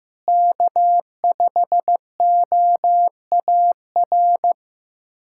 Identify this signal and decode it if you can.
Morse code, heard as a keyed tone, about 15 wpm, 700 Hz. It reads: K5OAR
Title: USA Callsign DE Callsign - 5 Characters - 15wpm